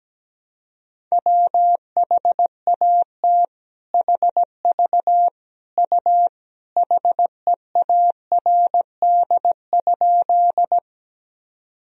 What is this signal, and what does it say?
Morse code, heard as a keyed tone, about 17 wpm, 700 Hz. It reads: WHAT HV U HEARD?
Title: Sentences from Top 400 Words and Abbreviations 17wpm